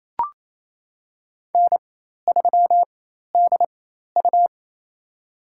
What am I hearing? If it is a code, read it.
Morse code, heard as a keyed tone, about 28 wpm, spaced as for 13 wpm, 700 Hz. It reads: N3DU
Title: USA Call Signs - 4 Characters - 4x Character Spacing - 28wpm